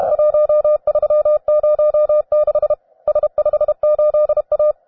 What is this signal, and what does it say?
Morse code, heard as a keyed tone, about 32 wpm, 600 Hz. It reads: 1306 S58A
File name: cw advanced